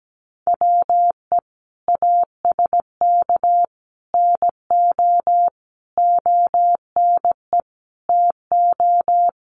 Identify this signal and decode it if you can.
Morse code, heard as a keyed tone, about 17 wpm, 700 Hz. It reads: WE ASK NO ONE TO
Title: Sentences from Top 1500 Words 17wpm